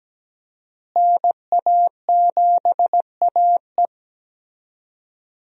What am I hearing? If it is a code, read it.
Morse code, heard as a keyed tone, about 17 wpm, 700 Hz. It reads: NA7AE
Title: USA Call Signs - 5 Characters - Repeated 2x - 17wpm